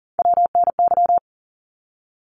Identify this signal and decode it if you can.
Morse code, heard as a keyed tone, about 40 wpm, 700 Hz. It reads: WNY
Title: ARRL Sections 40wpm